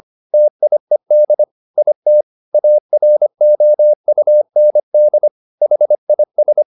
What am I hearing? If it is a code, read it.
Morse code, heard as a keyed tone, about 25 wpm, 600 Hz. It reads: TIED IT AROUND HIS
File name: the old man and the sea part 1 25wpm